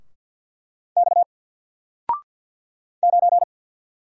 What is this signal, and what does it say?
Morse code, heard as a keyed tone, about 48 wpm, 700 Hz. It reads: X 9
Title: Single Letter-Number Alt 50wpm